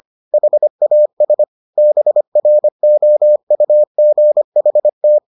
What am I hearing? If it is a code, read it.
Morse code, heard as a keyed tone, about 25 wpm, 600 Hz. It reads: HAS BROUGHT